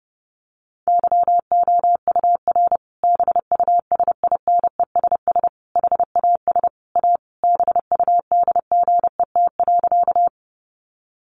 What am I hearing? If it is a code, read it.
Morse code, heard as a keyed tone, about 30 wpm, 700 Hz. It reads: YOUR BUSINESS HAS A BUDGET.
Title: Sentences from Top 1200 Words 30wpm